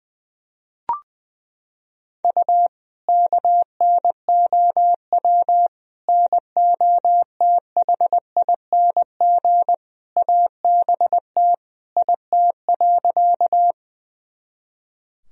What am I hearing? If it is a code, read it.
Morse code, heard as a keyed tone, about 20 wpm, 700 Hz. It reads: U KNOW NOTHING ABT IT.
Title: Sentences from Top 500 Words and Abbreviations 20wpm